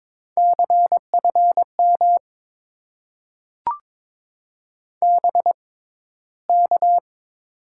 Morse code, heard as a keyed tone, about 21 wpm, spaced as for 13 wpm, 700 Hz. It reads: CFM BK
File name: Top 90 QSO Elements - 6x Character Spacing - 22wpm